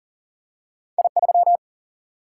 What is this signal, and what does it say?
Morse code, heard as a keyed tone, about 39 wpm, 700 Hz. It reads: I3
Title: Single-Letter Single-Number 40wpm